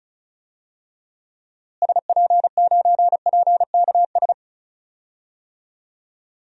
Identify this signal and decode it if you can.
Morse code, heard as a keyed tone, about 36 wpm, 700 Hz. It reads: SP9PKS
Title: DXCC-100 Call Signs - 6 Characters - Repeated 2x - 35wpm